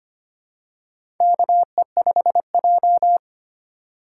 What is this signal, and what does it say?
Morse code, heard as a keyed tone, about 25 wpm, 700 Hz. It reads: KE5J
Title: USA Call Signs - 4 Characters - 3x Character Spacing - 25wpm